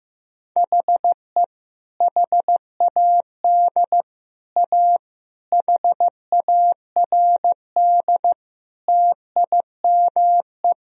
Morse code, heard as a keyed tone, about 15 wpm, 700 Hz. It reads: HE HAD A HARD TIME
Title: Sentences from Top 1000 Words - Review Part 1 - 15wpm